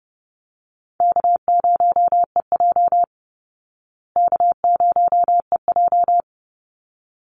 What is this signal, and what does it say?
Morse code, heard as a keyed tone, about 30 wpm, 700 Hz. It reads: K0EJ K0EJ
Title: Easy Call Signs Repeated 2x - 30wpm